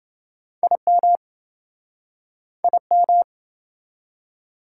Morse code, heard as a keyed tone, about 28 wpm, 700 Hz. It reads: IM IM